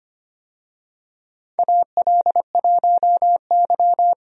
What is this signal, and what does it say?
Morse code, heard as a keyed tone, about 25 wpm, 700 Hz. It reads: AL1Y
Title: USA Call Signs - 4 Characters - Repeated 3x - 25wpm